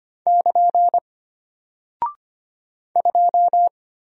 Morse code, heard as a keyed tone, about 25 wpm, 700 Hz. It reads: <KN> 2
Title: Single Character (Letter:Number:Punctuation:Prosign) Alt 25wpm